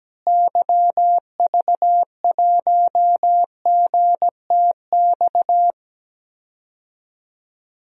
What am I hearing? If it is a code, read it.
Morse code, heard as a keyed tone, about 17 wpm, 700 Hz. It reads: YV1GTX
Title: DXCC-100 Call Signs - 6 Characters - Repeated 2x - 17wpm